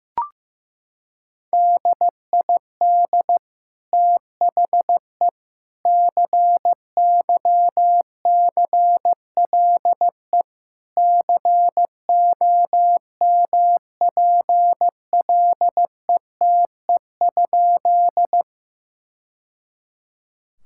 Morse code, heard as a keyed tone, about 15 wpm, 700 Hz. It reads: DID THE CYCLE COMPLETE?
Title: Sentences from Top 1000 Words - Review Part 6 - 15wpm